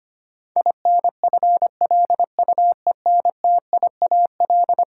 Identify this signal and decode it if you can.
Morse code, heard as a keyed tone, about 25 wpm, 700 Hz. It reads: INFLUENTIAL
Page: morse-code-ninja-practice-25wpm